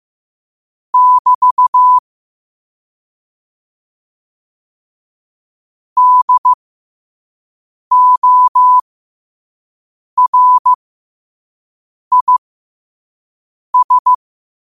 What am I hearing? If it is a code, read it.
Morse code, heard as a keyed tone, about 15 wpm, spaced as for 5 wpm, 1000 Hz. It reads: <BT> DORIS
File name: qotdfiles / 5wpm